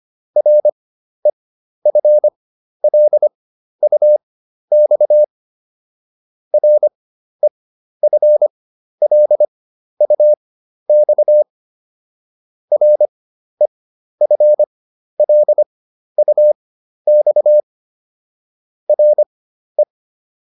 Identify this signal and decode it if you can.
Morse code, heard as a keyed tone, about 25 wpm, spaced as for 12 wpm, 600 Hz. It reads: REFLUX REFLUX REFLUX RE